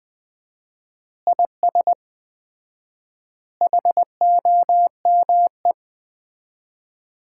Morse code, heard as a keyed tone, about 21 wpm, 700 Hz. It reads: IS HOME
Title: Common QSO Phrases - 4x Word-Spacing - 20wpm